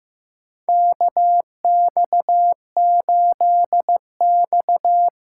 Morse code, heard as a keyed tone, about 16 wpm, 700 Hz. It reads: KX8X
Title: K1USN SST QSO Practice - 15wpm